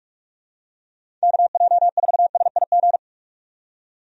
Morse code, heard as a keyed tone, about 46 wpm, 700 Hz. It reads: KJ4SIG
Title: USA Call Signs - 6 Characters - 45wpm